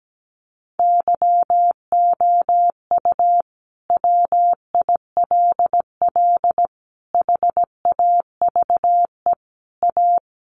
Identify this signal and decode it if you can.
Morse code, heard as a keyed tone, about 17 wpm, 700 Hz. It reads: YOU WILL HAVE A
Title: Sentences from Top 1100 Words 17wpm